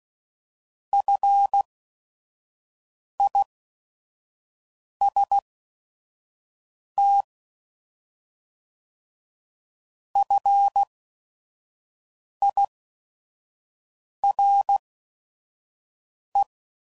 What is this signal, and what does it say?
Morse code, heard as a keyed tone, about 16 wpm, spaced as for 5 wpm, 800 Hz. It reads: FIST FIRE